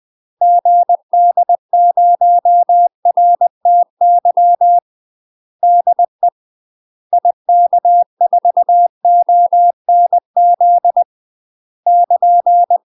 Morse code, heard as a keyed tone, about 20 wpm, 700 Hz. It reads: GD0RTY DE IK4ONZ <KN>
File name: UK MASTER QSOs TRAINING / 20WPM / UK MASTER QSOs 0066